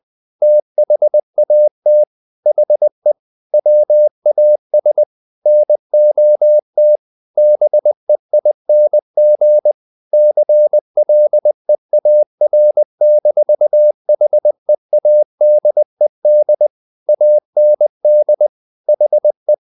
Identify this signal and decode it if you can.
Morse code, heard as a keyed tone, about 20 wpm, 600 Hz. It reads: THAT HE WAS NOT BEING CLEAR-HEADED AND HE
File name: the old man and the sea part 2 20wpm